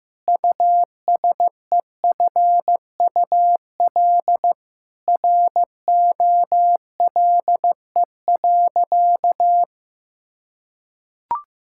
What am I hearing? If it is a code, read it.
Morse code, heard as a keyed tone, about 16 wpm, 700 Hz. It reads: USEFUL ROLE.
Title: Sentences from Top 1000 Words - Review Part 6 - 15wpm